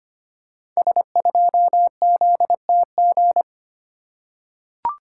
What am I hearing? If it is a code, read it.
Morse code, heard as a keyed tone, about 25 wpm, 700 Hz. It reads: S2ZTG
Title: Code Groups - 5 Letter:Numbers - 25wpm